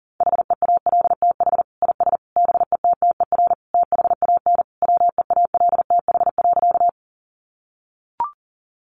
Morse code, heard as a keyed tone, about 40 wpm, 700 Hz. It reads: HEALTH IS BETTER THAN WEALTH.
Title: Sentences from Top 1200 Words 40wpm